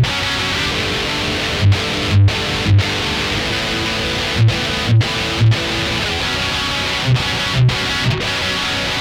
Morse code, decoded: none